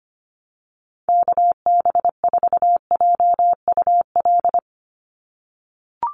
Morse code, heard as a keyed tone, about 25 wpm, 700 Hz. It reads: KB4JUL
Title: USA Call Signs 25wpm